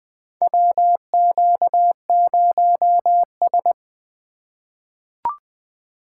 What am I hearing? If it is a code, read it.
Morse code, heard as a keyed tone, about 21 wpm, 700 Hz. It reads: WQ0S
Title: USA Call Signs - 4 Characters - 4x Character Spacing - 20wpm